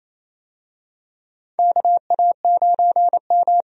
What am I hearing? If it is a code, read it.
Morse code, heard as a keyed tone, about 28 wpm, 700 Hz. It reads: KA9M
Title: USA Call Signs - 4 Characters - 28wpm